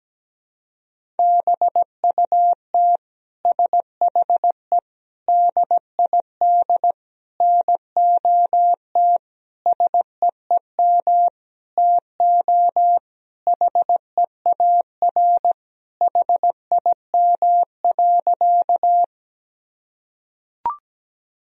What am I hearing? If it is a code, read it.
Morse code, heard as a keyed tone, about 17 wpm, 700 Hz. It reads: BUT SHE DID NOT SEEM TO HEAR HIM.
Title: Sentences from Top 1000 Words - Review Part 1 - 17wpm